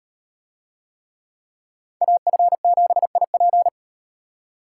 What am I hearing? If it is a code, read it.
Morse code, heard as a keyed tone, about 38 wpm, 700 Hz. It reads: AF7IP